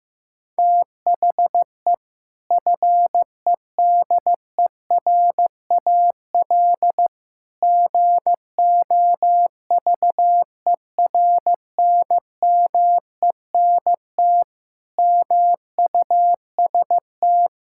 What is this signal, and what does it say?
Morse code, heard as a keyed tone, about 16 wpm, 700 Hz. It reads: THE FEDERAL GOVERNMENT MUST
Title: Sentences from Top 1000 Words - Review Part 5 - 15wpm